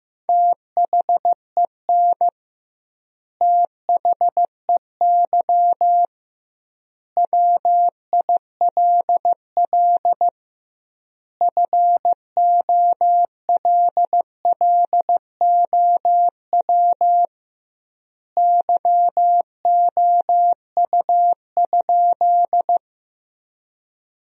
Morse code, heard as a keyed tone, about 15 wpm, 700 Hz. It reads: THEN THEY WILL FOLLOW YOU?
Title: 2x Word-Spacing - Sentences from Top 200 Words - 15wpm